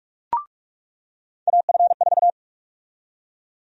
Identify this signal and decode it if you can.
Morse code, heard as a keyed tone, about 46 wpm, 700 Hz. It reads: AF4
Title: USA Call Sign Prefixes - 45wpm